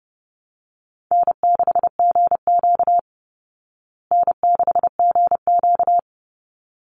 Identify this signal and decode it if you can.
Morse code, heard as a keyed tone, about 30 wpm, 700 Hz. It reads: N6GQ N6GQ
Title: Easy Call Signs Repeated 2x - 30wpm